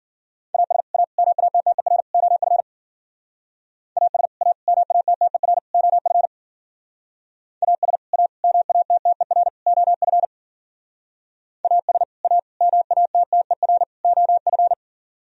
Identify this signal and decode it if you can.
Morse code, heard as a keyed tone, about 53 wpm, 700 Hz. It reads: AS A MATTER OF AS A MATTER OF AS A MATTER OF AS A MATTER OF